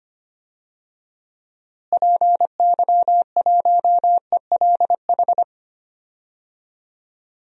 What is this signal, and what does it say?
Morse code, heard as a keyed tone, about 25 wpm, 700 Hz. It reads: PY1ELH
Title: DXCC-100 Call Signs - 6 Characters - Repeated 3x - 25wpm